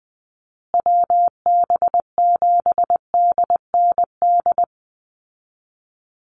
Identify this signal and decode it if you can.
Morse code, heard as a keyed tone, about 20 wpm, 700 Hz. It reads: WB7DND
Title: Call Signs Repeated 2x - 20wpm